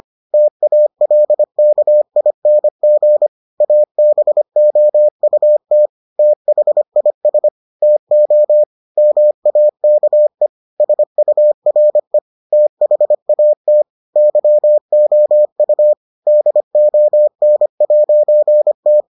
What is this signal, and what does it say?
Morse code, heard as a keyed tone, about 25 wpm, 600 Hz. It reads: TALKING ABOUT THIS TO MAKE SURE THAT YOU DON'T